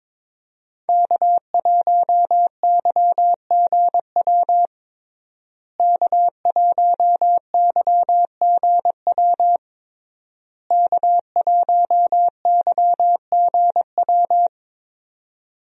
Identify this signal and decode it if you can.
Morse code, heard as a keyed tone, about 22 wpm, 700 Hz. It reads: K1YGW K1YGW K1YGW
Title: USA Call Signs - 5 Characters - Repeated 3x - 22wpm